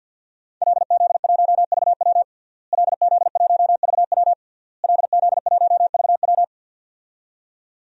Morse code, heard as a keyed tone, about 52 wpm, 700 Hz. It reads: LZ1VW LZ1VW LZ1VW
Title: DXCC-100 Call Signs - 5 Characters - Repeated 3x - 50wpm